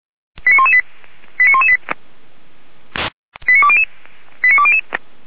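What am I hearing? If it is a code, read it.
Five-tone selective call (ZVEI), heard as quick runs of five tones, tones of about 70 ms. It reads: 89118 89118 89200 89200
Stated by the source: ZVEI five-tone selective call for a fire brigade, received on an SDR.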